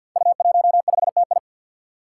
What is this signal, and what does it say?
Morse code, heard as a keyed tone, about 50 wpm, 700 Hz. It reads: U15TI
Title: Five Challenging Letter-Number Combinations 50wpm